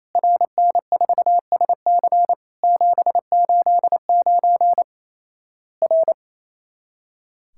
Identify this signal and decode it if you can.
Morse code, heard as a keyed tone, about 27 wpm, 700 Hz. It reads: RN4SC 789 R
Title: Worked All Europe DX Contest - QTC Exchange - 28wpm